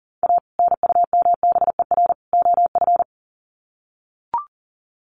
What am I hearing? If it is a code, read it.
Morse code, heard as a keyed tone, about 40 wpm, 700 Hz. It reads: A NUMBER OF
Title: Sets of 3 Words 40wpm